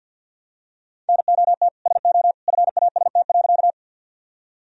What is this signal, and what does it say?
Morse code, heard as a keyed tone, about 50 wpm, 700 Hz. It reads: NOT SO FAST.
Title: Sentences from Top 1000 Words - Review Part 2 - 50wpm